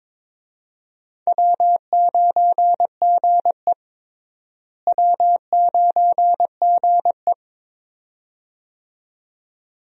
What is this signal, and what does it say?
Morse code, heard as a keyed tone, about 22 wpm, 700 Hz. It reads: W9GE W9GE
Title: USA Call Signs - 4 Characters - Repeated 2x - 22wpm